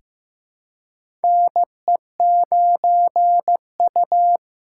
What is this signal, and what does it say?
Morse code, heard as a keyed tone, about 15 wpm, 700 Hz. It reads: NE9U
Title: K1USN SST QSO Practice - 15wpm